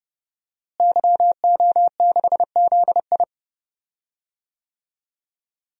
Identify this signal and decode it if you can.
Morse code, heard as a keyed tone, about 30 wpm, 700 Hz. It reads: YO6ZI
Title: DXCC-100 Call Signs - 5 Characters - Repeated 2x - 30wpm